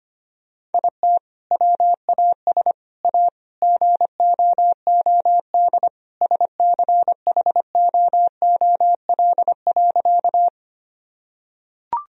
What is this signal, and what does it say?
Morse code, heard as a keyed tone, about 25 wpm, 700 Hz. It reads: IT WAS A GOOD SCHOOL.
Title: Sentences from Top 1000 Words - Review Part 1 - 25wpm